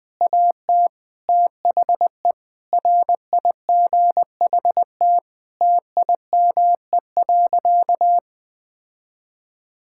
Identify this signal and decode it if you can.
Morse code, heard as a keyed tone, about 20 wpm, 700 Hz. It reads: AT THE RIGHT TIME.
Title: Sentences with 8-Word Phrases - 20wpm